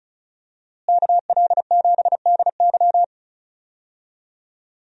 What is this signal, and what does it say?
Morse code, heard as a keyed tone, about 36 wpm, 700 Hz. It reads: KL7DY
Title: DXCC-100 Call Signs - 5 Characters - Repeated 2x - 35wpm